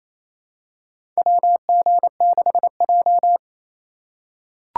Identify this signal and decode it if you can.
Morse code, heard as a keyed tone, about 28 wpm, 700 Hz. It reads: WG6J
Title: USA Call Signs - 4 Characters - 3x Character Spacing - 28wpm